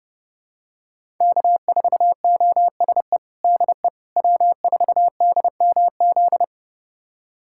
Morse code, heard as a keyed tone, about 30 wpm, 700 Hz. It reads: K4OSE DE W4DMZ
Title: USA Callsign DE Callsign - 5 Characters - 30wpm